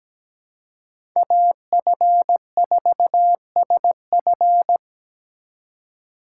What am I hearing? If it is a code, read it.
Morse code, heard as a keyed tone, about 17 wpm, 700 Hz. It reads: AF4SF
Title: USA Call Signs - 5 Characters - 2x Character Spacing - 17wpm